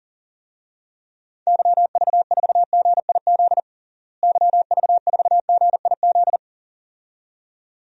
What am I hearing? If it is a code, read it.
Morse code, heard as a keyed tone, about 40 wpm, 700 Hz. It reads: YV4GIZ YV4GIZ
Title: DXCC-100 Call Signs - 6 Characters - Repeated 2x - 40wpm